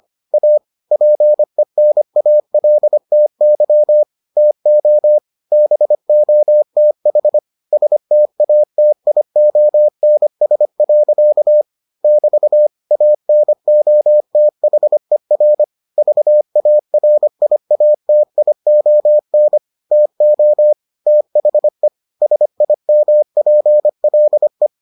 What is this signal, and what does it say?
Morse code, heard as a keyed tone, about 25 wpm, 600 Hz. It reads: A PENALTY TO BOTH STATIONS. <BT> ANOTHER VARIATION TO THE SIMPLE